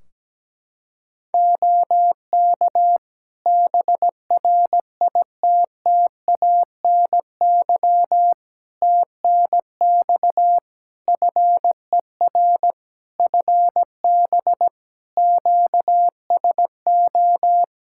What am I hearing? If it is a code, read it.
Morse code, heard as a keyed tone, about 17 wpm, 700 Hz. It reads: OK BRITTANY TNX FER FB QSO
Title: K1USN SST QSO Practice 17wpm